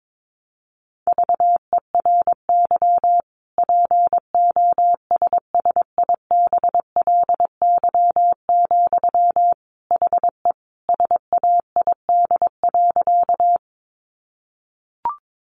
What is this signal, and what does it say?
Morse code, heard as a keyed tone, about 22 wpm, 700 Hz. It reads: VERY POSSIBLY, HE SAID.
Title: Sentences from Top 1300 Words 22wpm